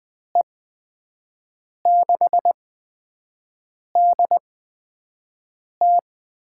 Morse code, heard as a keyed tone, about 21 wpm, spaced as for 5 wpm, 700 Hz. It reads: E6DT
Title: USA Call Signs - 8x Character Spacing - 20wpm